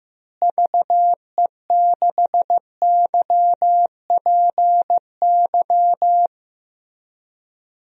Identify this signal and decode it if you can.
Morse code, heard as a keyed tone, about 15 wpm, 700 Hz. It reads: VE6YPY
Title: DXCC-100 Call Signs - 6 Characters - Repeated 2x - 15wpm